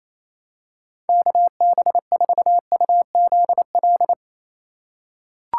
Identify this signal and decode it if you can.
Morse code, heard as a keyed tone, about 28 wpm, 700 Hz. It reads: KB4UZL
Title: USA Call Signs - 6 Characters - 4x Character Spacing - 28wpm